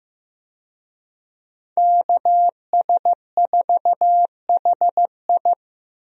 Morse code, heard as a keyed tone, about 15 wpm, 700 Hz. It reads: KS4HI
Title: USA Call Signs - 5 Characters - Repeated 3x - 15wpm